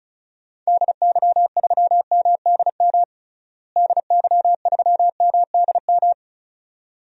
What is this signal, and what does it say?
Morse code, heard as a keyed tone, about 36 wpm, 700 Hz. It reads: DY3MDM DY3MDM
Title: DXCC-100 Call Signs - 6 Characters - Repeated 2x - 35wpm